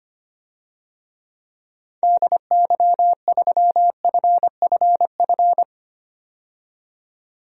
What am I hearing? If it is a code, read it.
Morse code, heard as a keyed tone, about 25 wpm, 700 Hz. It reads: DY3FFF
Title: DXCC-100 Call Signs - 6 Characters - 25wpm